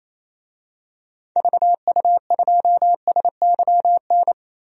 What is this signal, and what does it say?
Morse code, heard as a keyed tone, about 28 wpm, 700 Hz. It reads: VU2SYN
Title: DXCC-100 Call Signs - 6 Characters - 28wpm